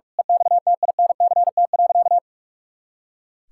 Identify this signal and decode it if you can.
Morse code, heard as a keyed tone, about 42 wpm, 700 Hz. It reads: EXTINCT.
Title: Pangram Challenge v2 - 45wpm